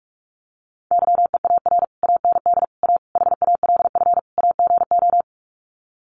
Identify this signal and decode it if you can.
Morse code, heard as a keyed tone, about 44 wpm, 700 Hz. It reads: YEAR AND A HALF AGO
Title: Sets of 5 Words Encore 45wpm